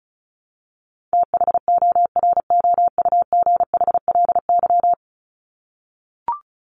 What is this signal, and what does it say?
Morse code, heard as a keyed tone, about 34 wpm, 700 Hz. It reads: THOROUGHLY
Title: LY Suffix - Top 100 Words with Base Words 35wpm